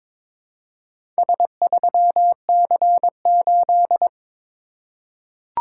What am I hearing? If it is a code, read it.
Morse code, heard as a keyed tone, about 22 wpm, 690 Hz. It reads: S3C8
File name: practice / 22wpm / Four Letter-Number Combinations - 22wpm